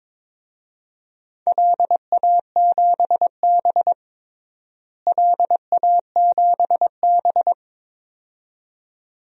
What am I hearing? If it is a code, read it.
Morse code, heard as a keyed tone, about 21 wpm, 700 Hz. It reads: LA7B LA7B
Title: DXCC-100 Call Signs - 4 Characters - Repeated 2x - 22wpm